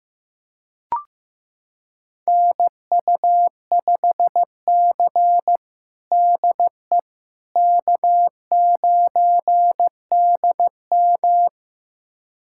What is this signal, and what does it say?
Morse code, heard as a keyed tone, about 16 wpm, 700 Hz. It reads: NU5C DE K9DM
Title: USA Callsign DE Callsign - 4 Characters - 15wpm